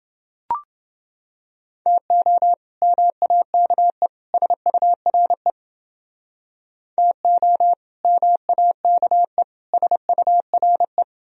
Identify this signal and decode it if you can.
Morse code, heard as a keyed tone, about 29 wpm, 700 Hz. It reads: TO MAKE SURE TO MAKE SURE